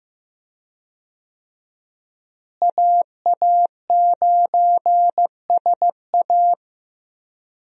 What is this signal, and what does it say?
Morse code, heard as a keyed tone, about 15 wpm, 700 Hz. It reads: AA9SA
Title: USA Call Signs - 5 Characters - Repeated 2x - 15wpm